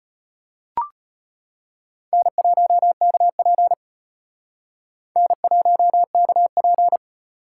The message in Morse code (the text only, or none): N1KP N1KP